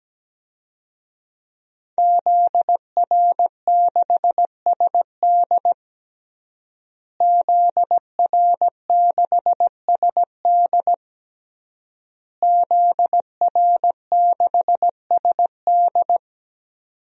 Morse code, heard as a keyed tone, about 17 wpm, 700 Hz. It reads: ZR6SD ZR6SD ZR6SD
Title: DXCC-100 Call Signs - 5 Characters - Repeated 3x - 17wpm